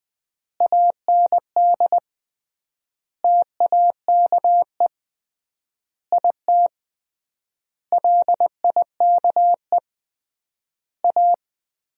Morse code, heard as a keyed tone, about 21 wpm, 700 Hz. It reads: AND TAKE IT LIKE A
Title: 3x Word-Spacing - Sentences from Top 300 Words - 20wpm